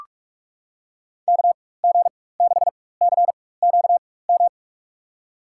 Morse code, heard as a keyed tone, about 46 wpm, spaced as for 21 wpm, 700 Hz. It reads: KG6CQM
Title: USA Call Signs - 6 Characters - 4x Character Spacing - 45wpm